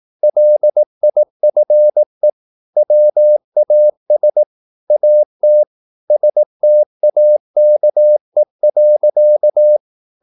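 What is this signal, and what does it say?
Morse code, heard as a keyed tone, about 18 wpm, 600 Hz. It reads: LIFE WAS AT STAKE.
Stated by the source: Morse practice at 18 wpm, English text, from CW med Gnister.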